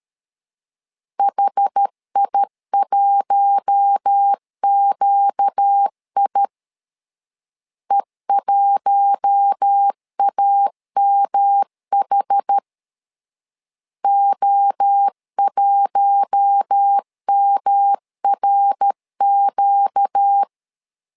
Morse code, heard as a keyed tone, about 13 wpm, 800 Hz. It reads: HI1QI E1AMH O1MRQ
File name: LEZIONI / 4 Q1 12wpm 2028